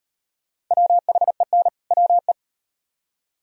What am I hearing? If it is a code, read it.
Morse code, heard as a keyed tone, about 34 wpm, 700 Hz. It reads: WHEN WE